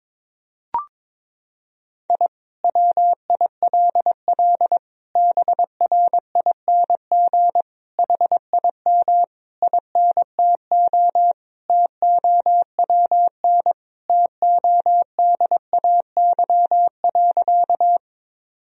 Morse code, heard as a keyed tone, about 22 wpm, 700 Hz. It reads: I WILL BRING HIM INTO TOWN TODAY.
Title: Sentences from Top 1000 Words - Review Part 3 - 22wpm